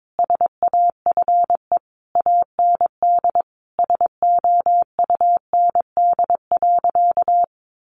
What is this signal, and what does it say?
Morse code, heard as a keyed tone, about 22 wpm, 700 Hz. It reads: SAFE AND SOUND.
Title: Sentences from Top 900 Words 22wpm